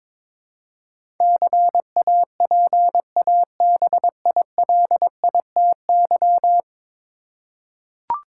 Morse code, heard as a keyed tone, about 22 wpm, 700 Hz. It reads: CAPABILITY
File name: practice / 22wpm / 10 Letter Words - First 3 Letters Given - 22wpm